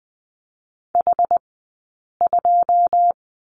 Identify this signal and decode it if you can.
Morse code, heard as a keyed tone, about 20 wpm, spaced as for 8 wpm, 700 Hz. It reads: H2
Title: 171 Lesson 28d - Number 9 - Two Characters 20wpm